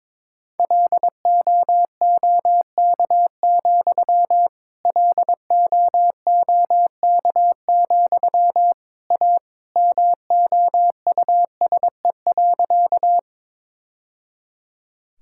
Morse code, heard as a keyed tone, about 22 wpm, 700 Hz. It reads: LOOK, LOOK, A MOUSE.
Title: Sentences from Top 1000 Words - Review Part 5 - 22wpm